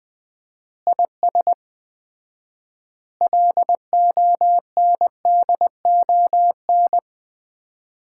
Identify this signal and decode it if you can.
Morse code, heard as a keyed tone, about 20 wpm, 700 Hz. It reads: IS LONDON
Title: Common QSO Phrases - 4x Word-Spacing - 20wpm